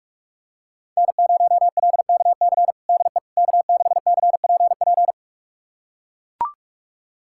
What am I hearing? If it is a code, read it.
Morse code, heard as a keyed tone, about 41 wpm, 700 Hz. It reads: N0LKC DE K6CPP
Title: USA Callsign DE Callsign - 5 Characters - 45wpm